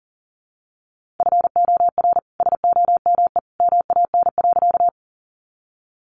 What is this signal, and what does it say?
Morse code, heard as a keyed tone, about 40 wpm, 700 Hz. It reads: FOR SOME MAN.
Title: Sentences from Top 200 Words 40wpm